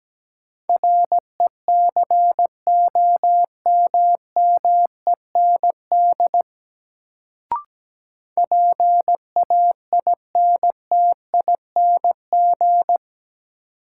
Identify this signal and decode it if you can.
Morse code, heard as a keyed tone, about 17 wpm, 700 Hz. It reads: RECOMMEND PAINTING
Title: Sentences from Top 1200 Words - New Word Drill Alt - 17wpm